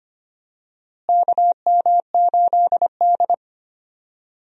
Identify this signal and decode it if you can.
Morse code, heard as a keyed tone, about 25 wpm, 700 Hz. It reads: KM8D
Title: USA Call Signs - 4 Characters - Repeated 2x - 25wpm